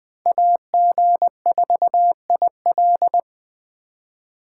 Morse code, heard as a keyed tone, about 24 wpm, 700 Hz. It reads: AG4IL
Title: Speed Racing - USA 2x2 Call Signs 20wpm